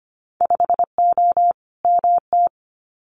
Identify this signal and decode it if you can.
Morse code, heard as a keyed tone, about 25 wpm, 700 Hz. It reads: 5O MT
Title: Winter Field Day Exchanges Encore 25wpm